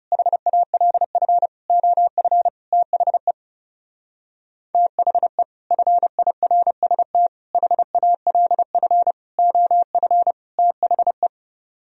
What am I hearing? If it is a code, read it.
Morse code, heard as a keyed tone, about 36 wpm, 700 Hz. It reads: HALF OF THE THE FIRST HALF OF THE
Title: Speed-Racing - Sets of 5 Words 30wpm